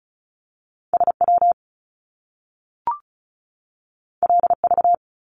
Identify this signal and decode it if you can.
Morse code, heard as a keyed tone, about 36 wpm, 700 Hz. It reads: SW LV
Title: Top Two Letter Combinations in Words 35wpm